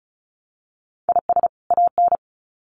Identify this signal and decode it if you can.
Morse code, heard as a keyed tone, about 34 wpm, 700 Hz. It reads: IS AN
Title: Sets of 2 Words 35wpm